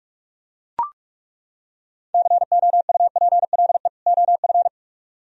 Morse code, heard as a keyed tone, about 49 wpm, 700 Hz. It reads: COUPLE OF